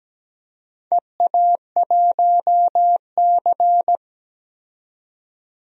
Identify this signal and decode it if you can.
Morse code, heard as a keyed tone, about 17 wpm, 700 Hz. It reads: EA1C
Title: DXCC-100 Call Signs - 4 Characters - 17wpm